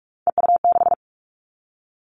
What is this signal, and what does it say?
Morse code, heard as a keyed tone, about 46 wpm, 700 Hz. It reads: EU6
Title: Callsign Prefixes for Top 50 DXCC - 45wpm